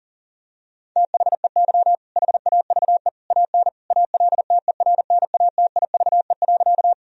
Morse code, heard as a keyed tone, about 37 wpm, 700 Hz. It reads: THEY HAVE AN ALTERNATIVE.
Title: Sentences from Top 1000 Words - Review Part 4 - 40wpm